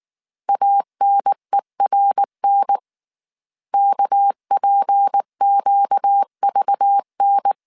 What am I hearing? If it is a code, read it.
Morse code, heard as a keyed tone, about 19 wpm, 800 Hz. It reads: ANERN KPQVN
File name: CW / DET / 1 6 1500C S A 18wpm 2255